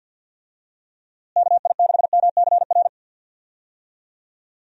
Morse code, heard as a keyed tone, about 51 wpm, 700 Hz. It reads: KI6MCR
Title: USA Call Signs - 6 Characters - Repeated 2x - 50wpm